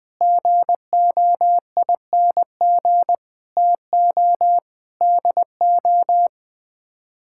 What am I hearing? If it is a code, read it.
Morse code, heard as a keyed tone, about 20 wpm, 700 Hz. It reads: GOING TO DO
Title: Sets of 8 Words - 20wpm